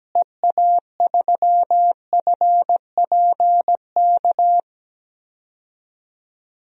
Morse code, heard as a keyed tone, about 17 wpm, 700 Hz. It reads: EA3FPK
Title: DXCC-100 Call Signs - 6 Characters - Repeated 2x - 17wpm